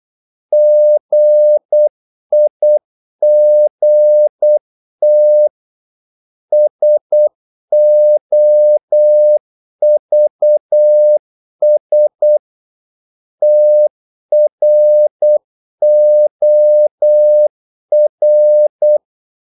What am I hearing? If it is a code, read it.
Morse code, heard as a keyed tone, about 8 wpm, 600 Hz. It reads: GIGT SOVS TROR
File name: Korte ord DK 8wpm